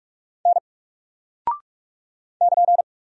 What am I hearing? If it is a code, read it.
Morse code, heard as a keyed tone, about 44 wpm, 700 Hz. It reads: N <KN>
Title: Single Character (Letter:Number:Punctuation:Prosign) 45wpm